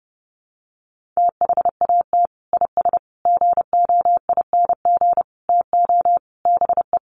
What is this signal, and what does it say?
Morse code, heard as a keyed tone, about 30 wpm, 700 Hz. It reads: THAT IS GOING TO BE
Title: Sets of 5 Words Encore2 30wpm